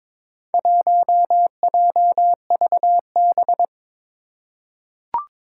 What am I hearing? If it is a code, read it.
Morse code, heard as a keyed tone, about 22 wpm, 700 Hz. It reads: 1JVB
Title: Four Challenging Letter-Number Combinations 22wpm